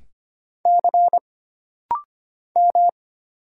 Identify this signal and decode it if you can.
Morse code, heard as a keyed tone, about 25 wpm, 700 Hz. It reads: C M
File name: Single Character (Letter:Number:Punctuation:Prosign) - Rapid-Fire - 25wpm